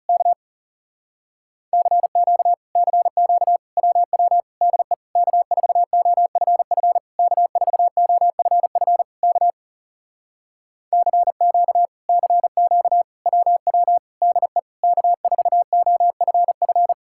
Morse code, heard as a keyed tone, about 37 wpm, 700 Hz. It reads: K CQ CQ WW DE K4OFF K4OFF K CQ CQ WW DE K4OFF
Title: Speed-Racing Common QSO Phrases Encore - 30wpm